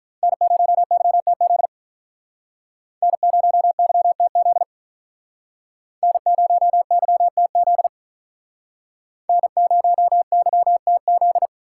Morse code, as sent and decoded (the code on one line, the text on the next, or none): -. ----- -.-- - --.. / -. ----- -.-- - --.. / -. ----- -.-- - --.. / -. ----- -.-- - --..
N0YTZ N0YTZ N0YTZ N0YTZ